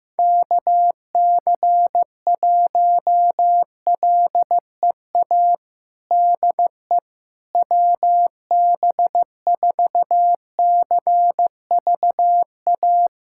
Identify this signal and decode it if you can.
Morse code, heard as a keyed tone, about 15 wpm, 700 Hz. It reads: KC1LEA DE WB4CVA
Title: USA Callsign DE Callsign 15wpm